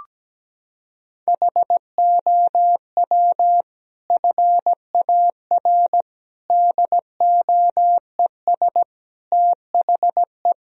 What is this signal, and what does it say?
Morse code, heard as a keyed tone, about 17 wpm, 700 Hz. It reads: HOW FAR DOES THE
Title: Sentences from Top 1000 Words - Review Part 4 - 17wpm